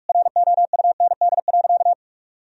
Morse code, heard as a keyed tone, about 42 wpm, 700 Hz. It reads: ROUND.
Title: Sentences from Top 1000 Words - Review Part 1 - 45wpm